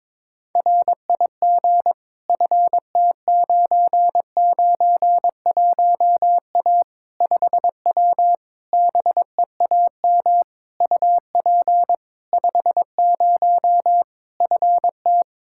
Morse code, heard as a keyed tone, about 21 wpm, 700 Hz. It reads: RIG FT991A 5W BEAM UP 50 FT
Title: Practice QSO - Exchange 2 - Info - 22wpm